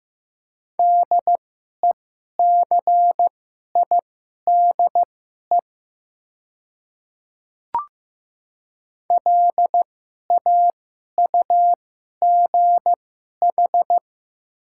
Morse code, heard as a keyed tone, about 15 wpm, spaced as for 11 wpm, 690 Hz. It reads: DECIDE LAUGH
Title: 2x Character Spacing - Top 500 Words - 15wpm